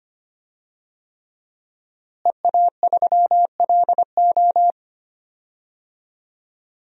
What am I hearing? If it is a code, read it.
Morse code, heard as a keyed tone, about 25 wpm, 700 Hz. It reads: EA3LO
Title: DXCC-100 Call Signs - 5 Characters - Repeated 3x - 25wpm